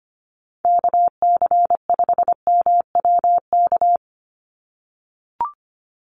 Morse code, heard as a keyed tone, about 25 wpm, 700 Hz. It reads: KC5MWK
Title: USA Call Signs 25wpm